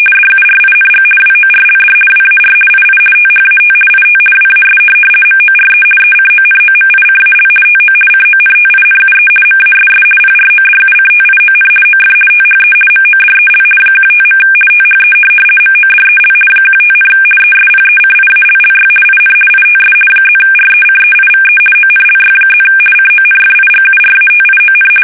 RTTY, about 50 baud. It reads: DURING WORLD WAR II THAT IT WAS MORE ECONOMICAL TO CONTINUE MASS PRODUCTION OF THE MODEL 15. THE MODEL 15, ESPECIALLY IN ITS +RECEIVE ONLY+ CONFIGURATION
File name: fsk850-50B / teletype-corp